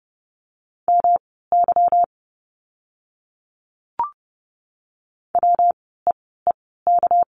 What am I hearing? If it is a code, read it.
Morse code, heard as a keyed tone, about 30 wpm, spaced as for 17 wpm, 700 Hz. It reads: MY WEEK
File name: course / 30:10wpm / 230 Lesson 37f - Letter x - Words 30wpm